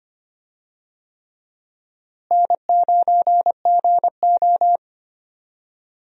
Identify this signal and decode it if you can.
Morse code, heard as a keyed tone, about 25 wpm, 700 Hz. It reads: N9GO
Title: USA Call Signs - 4 Characters - Repeated 3x - 25wpm